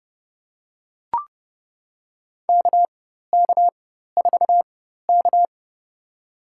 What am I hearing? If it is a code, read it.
Morse code, heard as a keyed tone, about 30 wpm, spaced as for 14 wpm, 700 Hz. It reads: KK4K
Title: USA Call Signs - 4 Characters - 4x Character Spacing - 30wpm